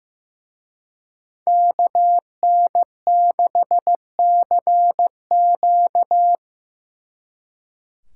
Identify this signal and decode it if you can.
Morse code, heard as a keyed tone, about 18 wpm, 700 Hz. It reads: KN6CQ
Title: Speed Racing - USA 2x2 Call Signs 15wpm